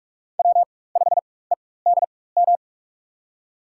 Morse code, heard as a keyed tone, about 44 wpm, spaced as for 21 wpm, 700 Hz. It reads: W5EDM
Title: USA Call Signs - 5 Characters - 4x Character Spacing - 45wpm